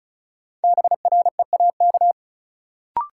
Morse code, heard as a keyed tone, about 36 wpm, 700 Hz. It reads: BREAK
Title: Sets of 4 Words - New Word Drill - 35wpm